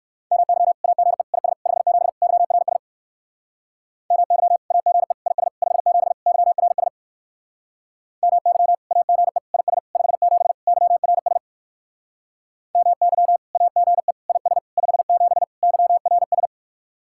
Morse code, heard as a keyed tone, about 59 wpm, 700 Hz. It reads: MY AGE IS 57 YRS MY AGE IS 57 YRS MY AGE IS 57 YRS MY AGE IS 57 YRS